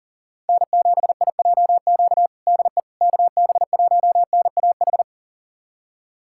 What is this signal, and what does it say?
Morse code, heard as a keyed tone, about 40 wpm, 700 Hz. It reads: N7IJQ DE KB1NAH
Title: USA Callsign DE Callsign 40wpm